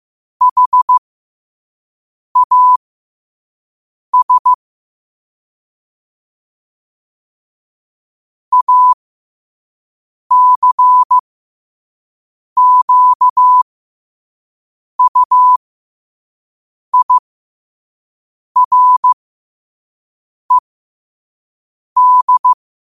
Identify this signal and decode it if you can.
Morse code, heard as a keyed tone, about 15 wpm, spaced as for 5 wpm, 1000 Hz. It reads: HAS ACQUIRED
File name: qotdfiles / 5wpm